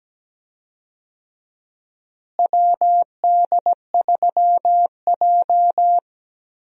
Morse code, heard as a keyed tone, about 17 wpm, 700 Hz. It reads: WD3J
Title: USA Call Signs - 4 Characters - Repeated 2x - 17wpm